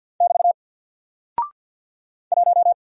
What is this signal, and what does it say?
Morse code, heard as a keyed tone, about 48 wpm, 700 Hz. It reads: <BT> 1
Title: Single Character (Letter:Number:Punctuation:Prosign) Alt 50wpm